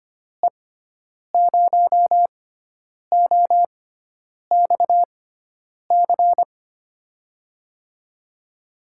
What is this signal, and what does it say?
Morse code, heard as a keyed tone, about 25 wpm, spaced as for 9 wpm, 700 Hz. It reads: E0OXC
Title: USA Call Signs - 6x Character Spacing - 25wpm